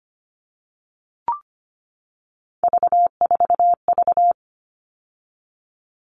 Morse code, heard as a keyed tone, about 25 wpm, 700 Hz. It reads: V4V
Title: All Cut Numbers - 3-Digit 25wpm